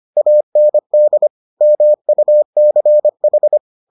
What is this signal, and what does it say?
Morse code, heard as a keyed tone, about 25 wpm, 600 Hz. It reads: AND MUCH